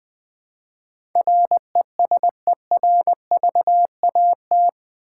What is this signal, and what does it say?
Morse code, heard as a keyed tone, about 20 wpm, 700 Hz. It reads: RESERVAT
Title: Build A Word - 4 Letters At A Time - 11-Letter Words - 20wpm